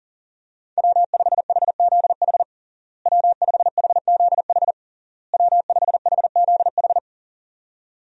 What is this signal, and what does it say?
Morse code, heard as a keyed tone, about 40 wpm, 700 Hz. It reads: W5HZH W5HZH W5HZH
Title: USA Call Signs - 5 Characters - Repeated 3x - 40wpm